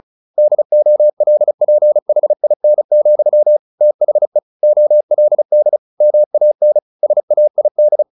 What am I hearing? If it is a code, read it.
Morse code, heard as a keyed tone, about 34 wpm, 600 Hz. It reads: DOLPHIN, THE OLD MAN SAID
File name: the old man and the sea / the old man and the sea 1 35wpm